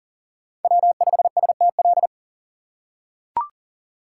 Morse code, heard as a keyed tone, about 40 wpm, 700 Hz. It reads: W5STL
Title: USA Call Signs - 5 Characters - 3x Character Spacing - 40wpm